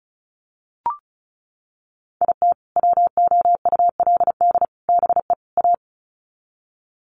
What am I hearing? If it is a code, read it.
Morse code, heard as a keyed tone, about 34 wpm, 700 Hz. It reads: IT WOULD BE A
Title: Sets of 4 Words 35wpm